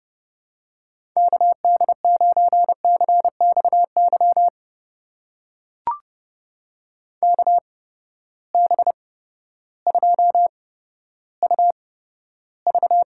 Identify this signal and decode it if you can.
Morse code, heard as a keyed tone, about 30 wpm, spaced as for 14 wpm, 700 Hz. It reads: KD9CXY KB2UV
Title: USA Call Signs - 6 Characters - 8x Character Spacing - 30wpm